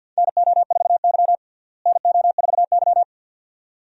Morse code, heard as a keyed tone, about 47 wpm, 700 Hz. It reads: NO4Y NO4Y
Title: USA Call Signs - 4 Characters - Repeated 2x - 50wpm